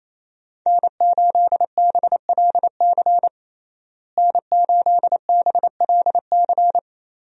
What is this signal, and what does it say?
Morse code, heard as a keyed tone, about 27 wpm, 700 Hz. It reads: N8BLC N8BLC
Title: USA Call Signs - 5 Characters - Repeated 2x - 28wpm